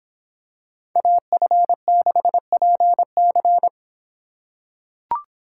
Morse code, heard as a keyed tone, about 26 wpm, 700 Hz. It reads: AF6PC